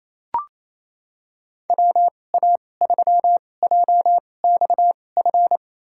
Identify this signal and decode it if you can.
Morse code, heard as a keyed tone, about 28 wpm, spaced as for 20 wpm, 700 Hz. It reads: WA3JXF
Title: USA Call Signs - 2x Character Spacing - 28wpm